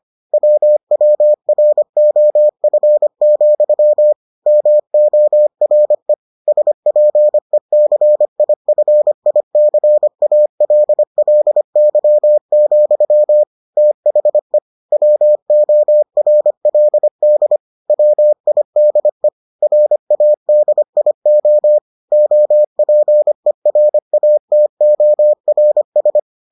Morse code, heard as a keyed tone, about 25 wpm, 600 Hz. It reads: WWROF, MORE SPECIFICALLY, THE WORLD WIDE RADIO OPERATORS